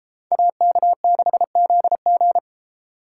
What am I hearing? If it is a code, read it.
Morse code, heard as a keyed tone, about 40 wpm, 700 Hz. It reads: AK6ZG